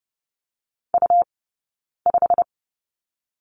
Morse code, heard as a keyed tone, about 30 wpm, 700 Hz. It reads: U 5
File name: Lesson 20d - Number 3 - Two Characters 30wpm